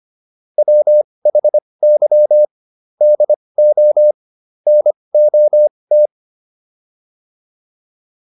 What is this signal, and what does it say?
Morse code, heard as a keyed tone, about 25 wpm, spaced as for 20 wpm, 600 Hz. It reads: WHY DO NOT